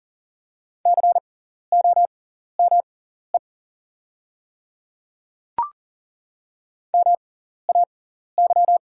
Morse code, heard as a keyed tone, about 40 wpm, spaced as for 14 wpm, 690 Hz. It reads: COME MAY
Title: 6x Character Spacing - Top 100 Words - 40wpm